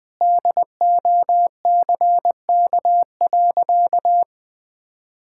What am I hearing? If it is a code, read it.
Morse code, heard as a keyed tone, about 20 wpm, 700 Hz. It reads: DOCK.
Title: Sentences with 2-Word Phrases - 20wpm